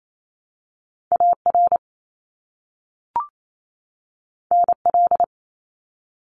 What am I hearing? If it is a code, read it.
Morse code, heard as a keyed tone, about 28 wpm, 700 Hz. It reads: AR NL
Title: ARRL Sections 28wpm